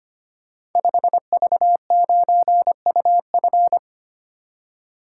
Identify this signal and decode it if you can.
Morse code, heard as a keyed tone, about 25 wpm, 700 Hz. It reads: 5V9UF
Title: Code Groups - 5 Letter:Numbers - 25wpm